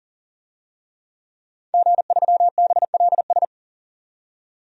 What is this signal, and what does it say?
Morse code, heard as a keyed tone, about 40 wpm, 700 Hz. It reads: G3BLS
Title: DXCC-100 Call Signs - 5 Characters - 40wpm